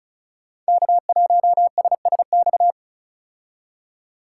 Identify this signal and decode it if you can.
Morse code, heard as a keyed tone, about 36 wpm, 700 Hz. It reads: K1SSX
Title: Code Groups - 5 Letter:Numbers - 35wpm